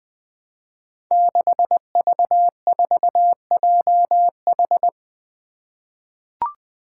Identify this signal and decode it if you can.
Morse code, heard as a keyed tone, about 21 wpm, 700 Hz. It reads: 6V4JH
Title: Five Challenging Letter-Number Combinations 20wpm